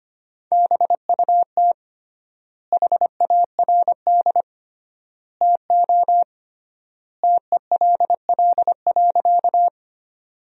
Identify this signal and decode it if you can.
Morse code, heard as a keyed tone, about 25 wpm, 700 Hz. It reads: BUT HARD TO TELL.
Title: 3x Word-Spacing - Sentences from Top 300 Words - 25wpm